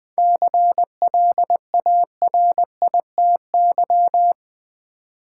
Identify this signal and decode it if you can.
Morse code, heard as a keyed tone, about 20 wpm, 700 Hz. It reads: CLARITY
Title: 7 Letter Words - First 3 Letters Given - 20wpm